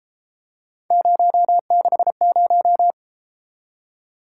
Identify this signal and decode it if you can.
Morse code, heard as a keyed tone, about 32 wpm, 700 Hz. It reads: 060